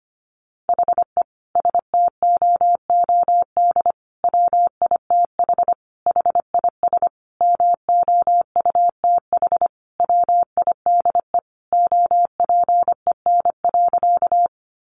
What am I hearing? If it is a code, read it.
Morse code, heard as a keyed tone, about 25 wpm, 700 Hz. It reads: HE STOOD WITH HIS MOUTH WIDE OPEN.
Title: Sentences from Top 1500 Words 25wpm